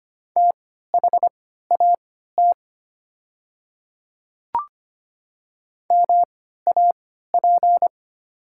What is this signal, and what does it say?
Morse code, heard as a keyed tone, about 25 wpm, spaced as for 14 wpm, 690 Hz. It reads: THAT MAP
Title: 3x Character Spacing - Top 500 Words - 25wpm